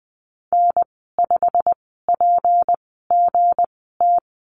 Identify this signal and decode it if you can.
Morse code, heard as a keyed tone, about 20 wpm, spaced as for 15 wpm, 700 Hz. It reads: N5PGT
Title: 161 Lesson 26g - Letter g - Call Signs 20wpm